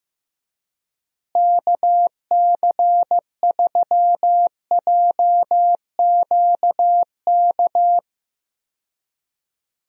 Morse code, heard as a keyed tone, about 16 wpm, 700 Hz. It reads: KC3JQK
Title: USA Call Signs - 6 Characters - Repeated 3x - 15wpm